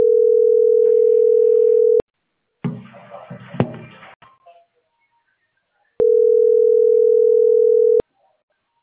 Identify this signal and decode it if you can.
Ringback tone (North America), 440 + 480 Hz.